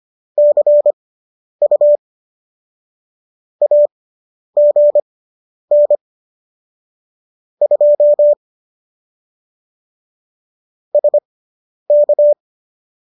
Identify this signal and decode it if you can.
Morse code, heard as a keyed tone, about 25 wpm, spaced as for 10 wpm, 590 Hz. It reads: CU AGN 2 SK